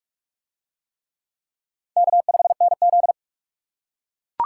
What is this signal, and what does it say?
Morse code, heard as a keyed tone, about 42 wpm, 700 Hz. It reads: K5NZ